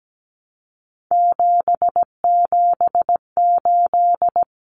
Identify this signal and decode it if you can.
Morse code, heard as a keyed tone, about 17 wpm, 700 Hz. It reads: 778
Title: Common Cut Numbers - 3-Digit 17wpm